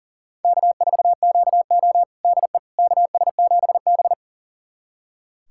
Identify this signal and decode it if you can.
Morse code, heard as a keyed tone, about 40 wpm, 700 Hz. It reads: K4QO DE KS7B
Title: USA Callsign DE Callsign - 4 Characters - 40wpm